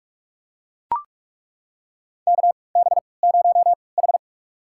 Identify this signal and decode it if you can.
Morse code, heard as a keyed tone, about 42 wpm, spaced as for 25 wpm, 700 Hz. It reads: KB0H
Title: USA Call Signs - 3x Character Spacing - 45wpm